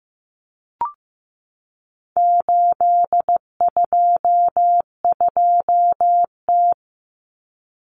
Morse code, heard as a keyed tone, about 15 wpm, 700 Hz. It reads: 822T
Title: Common Cut Numbers - 4-Digit 15wpm